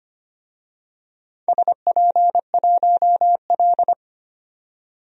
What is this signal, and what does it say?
Morse code, heard as a keyed tone, about 25 wpm, 700 Hz. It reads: SP1L
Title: DXCC-100 Call Signs - 4 Characters - 25wpm